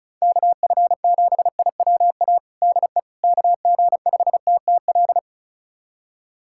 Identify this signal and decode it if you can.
Morse code, heard as a keyed tone, about 36 wpm, 700 Hz. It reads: KF7IWA DE KG5TTL
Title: USA Callsign DE Callsign 35wpm